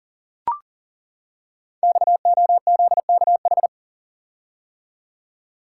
Morse code, heard as a keyed tone, about 38 wpm, 700 Hz. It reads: XOZKH
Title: Code Groups - 5 Letter:Numbers - 40wpm